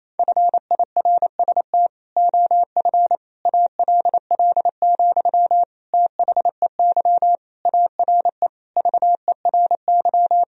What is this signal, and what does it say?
Morse code, heard as a keyed tone, about 28 wpm, 700 Hz. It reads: FIRST OF ALL, THEY ARE VERY
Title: Sentences from Top 1000 Words - Review Part 4 - 28wpm